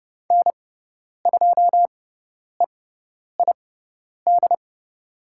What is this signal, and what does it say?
Morse code, heard as a keyed tone, about 30 wpm, spaced as for 10 wpm, 700 Hz. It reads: N2EID
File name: Lesson 16g - Letter c - Call Signs 30wpm